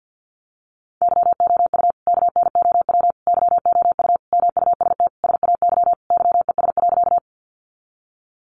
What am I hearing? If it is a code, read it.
Morse code, heard as a keyed tone, about 50 wpm, 700 Hz. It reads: YOU KNOW YOU MUST SAY YES.
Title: Sentences from Top 500 Words 50wpm